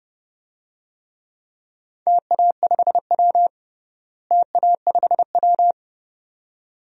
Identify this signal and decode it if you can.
Morse code, heard as a keyed tone, about 32 wpm, 700 Hz. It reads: TA5W TA5W
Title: DXCC-100 Call Signs - 4 Characters - Repeated 2x - 30wpm